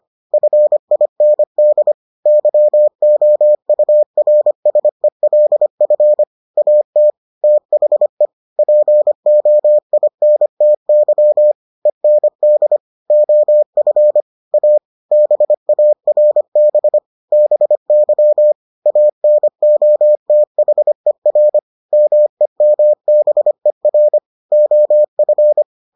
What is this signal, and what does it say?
Morse code, heard as a keyed tone, about 25 wpm, 600 Hz. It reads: FIND YOURSELF AT THE POINTY END OF A BARB BY ANOTHER MEMBER OF